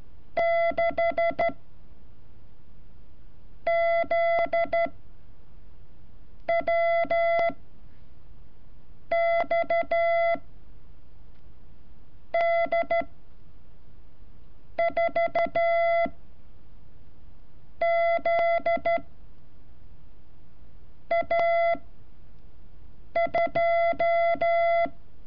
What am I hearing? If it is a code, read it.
Morse code, heard as a keyed tone, about 13 wpm, spaced as for 5 wpm, 680 Hz. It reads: 6 ZWX D 4 Z A2
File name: curso / ejercicio 31